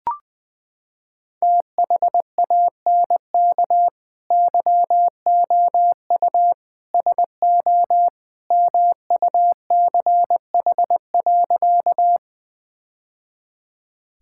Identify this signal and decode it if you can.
Morse code, heard as a keyed tone, about 20 wpm, 700 Hz. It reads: THANK YOU SO MUCH.
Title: Sentences with 2-Word Phrases - 20wpm